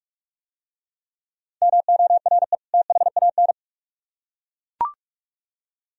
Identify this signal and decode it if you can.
Morse code, heard as a keyed tone, about 41 wpm, 700 Hz. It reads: MORE THAN